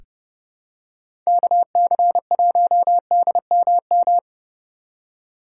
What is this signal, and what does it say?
Morse code, heard as a keyed tone, about 30 wpm, 700 Hz. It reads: KC1DMM
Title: USA Call Signs - 6 Characters - 3x Character Spacing - 30wpm